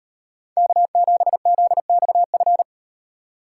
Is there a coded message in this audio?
Yes: Morse code from 0.6 to 2.6 s, heard as a keyed tone, about 35 wpm, 700 Hz. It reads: K7ZXF